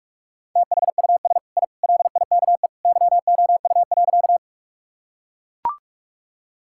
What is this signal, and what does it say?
Morse code, heard as a keyed tone, about 41 wpm, 700 Hz. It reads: THUS I LIKE YOU.
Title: Sentences from Top 1000 Words - Review Part 5 - 45wpm